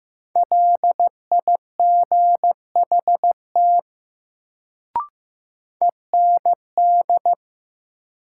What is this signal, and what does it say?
Morse code, heard as a keyed tone, about 15 wpm, 700 Hz. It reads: LIGHT END
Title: Sentences from Top 200 Words - New Word Drill - 15wpm